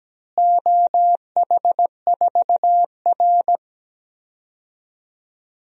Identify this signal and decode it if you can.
Morse code, heard as a keyed tone, about 17 wpm, 700 Hz. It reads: OH4R
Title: DXCC-100 Call Signs - 4 Characters - Repeated 3x - 17wpm